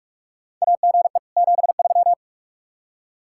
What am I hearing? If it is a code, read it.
Morse code, heard as a keyed tone, about 41 wpm, 700 Hz. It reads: AGE 73
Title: K1USN SST QSO Practice 45wpm